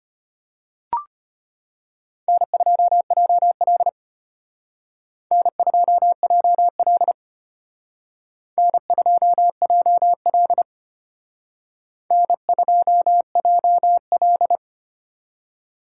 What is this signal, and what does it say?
Morse code, heard as a keyed tone, about 31 wpm, 700 Hz. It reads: N2JL N2JL N2JL N2JL